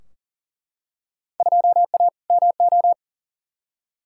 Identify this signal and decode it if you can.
Morse code, heard as a keyed tone, about 40 wpm, 700 Hz. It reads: 2A MO
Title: Field Day Exchanges Encore - 40wpm